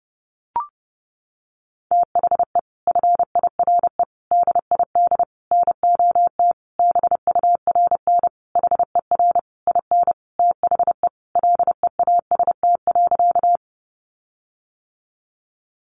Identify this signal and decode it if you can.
Morse code, heard as a keyed tone, about 30 wpm, 700 Hz. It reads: THE FIRE DID NOT BURN HER IN THE LEAST.
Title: Sentences from Top 1300 Words 30wpm